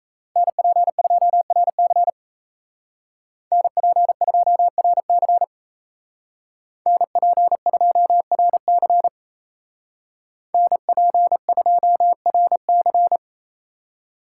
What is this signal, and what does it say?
Morse code, heard as a keyed tone, about 36 wpm, 700 Hz. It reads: NP2RC NP2RC NP2RC NP2RC